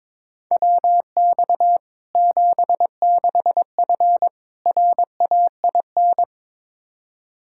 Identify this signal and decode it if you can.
Morse code, heard as a keyed tone, about 22 wpm, 700 Hz. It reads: WX 76F RAIN
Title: Practice QSO - Exchange 2 - Info - 22wpm